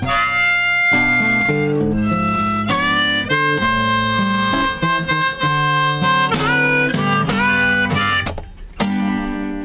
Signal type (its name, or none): dial-up modem handshake